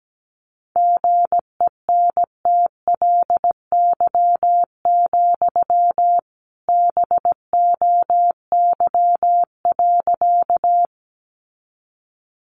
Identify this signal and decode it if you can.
Morse code, heard as a keyed tone, about 17 wpm, 700 Hz. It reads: GENTLY, BOY.
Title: Sentences from Top 1500 Words 17wpm